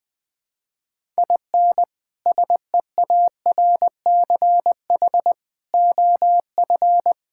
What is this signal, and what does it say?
Morse code, heard as a keyed tone, about 19 wpm, 700 Hz. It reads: IN SEARCH OF
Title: Sentences from Top 1000 Words - Review Part 5 - 20wpm